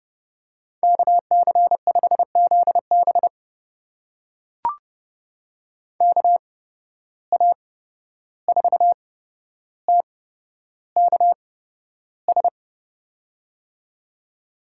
Morse code, heard as a keyed tone, about 30 wpm, spaced as for 12 wpm, 700 Hz. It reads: KC5ZB KA4TKS
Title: USA Call Signs - 8x Character Spacing - 30wpm